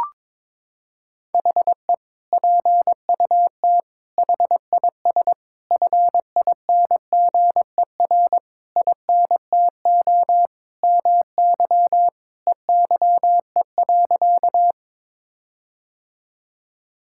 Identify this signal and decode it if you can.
Morse code, heard as a keyed tone, about 22 wpm, 700 Hz. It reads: HE PUT HIS FINGER INTO MY EYE.
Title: Sentences from Top 1600 Words 22wpm